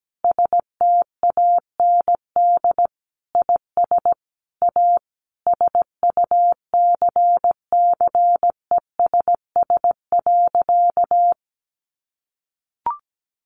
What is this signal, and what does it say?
Morse code, heard as a keyed tone, about 17 wpm, 700 Hz. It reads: STAND IS A SUCCESS.
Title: Additional Easy Phrases 17wpm